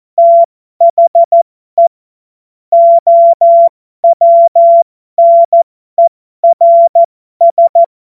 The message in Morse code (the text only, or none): THE OWNERS